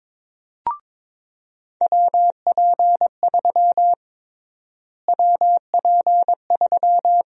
Morse code, heard as a keyed tone, about 22 wpm, 700 Hz. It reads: WP3 WP3
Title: USA Call Sign Prefixes - Repeated 2x - 22wpm